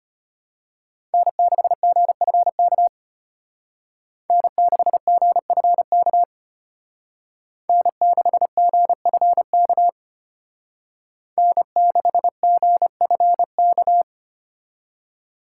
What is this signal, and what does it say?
Morse code, heard as a keyed tone, about 32 wpm, 700 Hz. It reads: N6GFK N6GFK N6GFK N6GFK